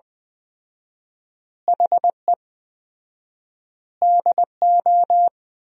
Morse code, heard as a keyed tone, about 20 wpm, 700 Hz. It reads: HE DO
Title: 4x Word-Spacing - Sentences from Top 100 Words - 20wpm